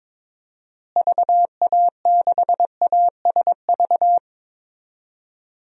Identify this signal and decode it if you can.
Morse code, heard as a keyed tone, about 22 wpm, 700 Hz. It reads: VA6ASV
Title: DXCC-100 Call Signs - 6 Characters - Repeated 2x - 22wpm